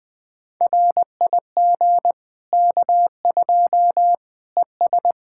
Morse code, heard as a keyed tone, about 20 wpm, 700 Hz. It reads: RIG K2 ES
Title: Practice QSO - Exchange 2 - Info - 20wpm